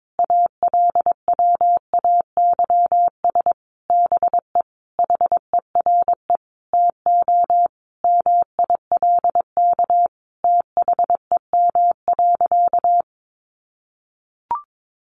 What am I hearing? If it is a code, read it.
Morse code, heard as a keyed tone, about 22 wpm, 700 Hz. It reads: ALWAYS BE HERE TO MILK THEM.
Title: Sentences from Top 1500 Words 22wpm